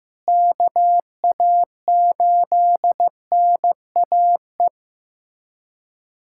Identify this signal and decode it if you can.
Morse code, heard as a keyed tone, about 15 wpm, 700 Hz. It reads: KA8NAE
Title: USA Callsign DE Callsign - 6 Characters - 15wpm